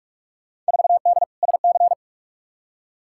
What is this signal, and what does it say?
Morse code, heard as a keyed tone, about 41 wpm, 700 Hz. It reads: 4D SC
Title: Field Day Exchanges Encore - 45wpm